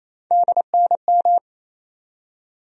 Morse code, heard as a keyed tone, about 28 wpm, 700 Hz. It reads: DNM
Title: USA Call Sign Postfixes - 3 Letters 28wpm